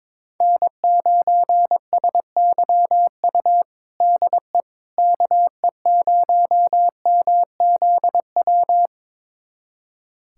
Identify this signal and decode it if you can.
Morse code, heard as a keyed tone, about 22 wpm, 700 Hz. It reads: N9SYU DE KE0MZW
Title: USA Callsign DE Callsign 22wpm